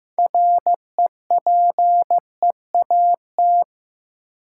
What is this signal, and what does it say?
Morse code, heard as a keyed tone, about 15 wpm, 700 Hz. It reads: REPEAT
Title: Sentences from Top 1600 Words - New Word Drill - 15wpm